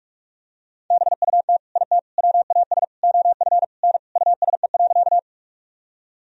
Morse code, heard as a keyed tone, about 46 wpm, 700 Hz. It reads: BUT IT WAS OF N USE.
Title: Sentences from Top 100 Words and Abbreviations 45wpm